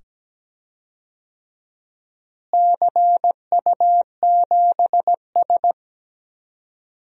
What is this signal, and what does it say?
Morse code, heard as a keyed tone, about 17 wpm, 700 Hz. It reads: CU7S
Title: DXCC-100 Call Signs - 4 Characters - Repeated 2x - 17wpm